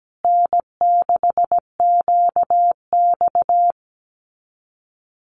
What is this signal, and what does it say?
Morse code, heard as a keyed tone, about 17 wpm, 700 Hz. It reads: N6QX
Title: Common QSO Phrases Encore 17wpm